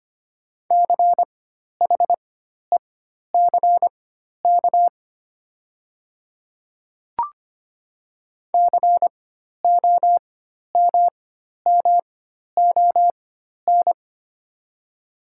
Morse code, heard as a keyed tone, about 25 wpm, spaced as for 12 wpm, 700 Hz. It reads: CHECK COMMON
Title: 4x Character Spacing - Top 500 Words - 25wpm